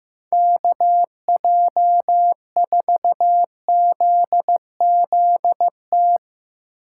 Morse code, heard as a keyed tone, about 15 wpm, 700 Hz. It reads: KJ4ZZT
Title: USA Call Signs - 2x Character Spacing - 15wpm